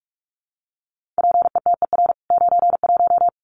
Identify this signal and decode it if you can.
Morse code, heard as a keyed tone, about 44 wpm, 700 Hz. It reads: PETER 91
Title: CWT Exchange 45wpm